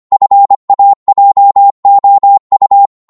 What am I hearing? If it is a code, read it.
Morse code, heard as a keyed tone, about 25 wpm, 800 Hz. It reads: FAJOU
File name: Morse / ltr-25wpm